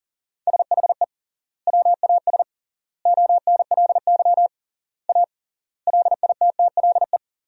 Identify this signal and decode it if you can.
Morse code, heard as a keyed tone, about 37 wpm, 700 Hz. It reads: SHE WAS ONLY A LITTLE
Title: 3x Word-Spacing - Sentences from Top 400 Words - 40wpm